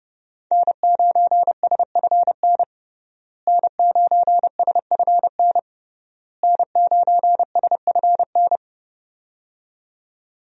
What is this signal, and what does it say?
Morse code, heard as a keyed tone, about 30 wpm, 700 Hz. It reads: N9SFN N9SFN N9SFN
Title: USA Call Signs - 5 Characters - Repeated 3x - 30wpm